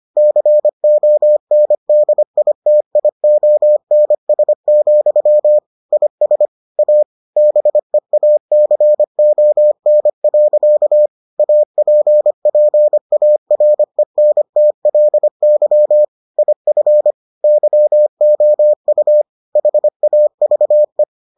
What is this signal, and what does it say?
Morse code, heard as a keyed tone, about 25 wpm, 600 Hz. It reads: CONDITIONS, IS A BEACON. APPARENTLY IF YOU HAVE